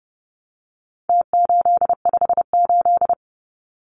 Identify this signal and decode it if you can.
Morse code, heard as a keyed tone, about 30 wpm, 700 Hz. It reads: T858
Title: Numbers - Contest Style - 4-Digit 30wpm